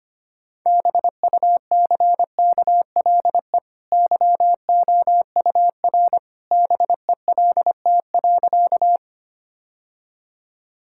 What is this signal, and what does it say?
Morse code, heard as a keyed tone, about 25 wpm, 700 Hz. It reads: BUCKLE YOUR BELT.
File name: Easy Sentences with Hints - 25wpm